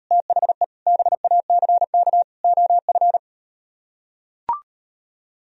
Morse code, heard as a keyed tone, about 38 wpm, 700 Hz. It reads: THE BACK OF